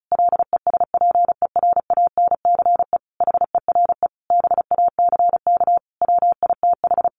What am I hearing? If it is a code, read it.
Morse code, heard as a keyed tone, about 34 wpm, 700 Hz. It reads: LESPERANCE HERE BACK WITH
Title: A Sound Of Thunder 35wpm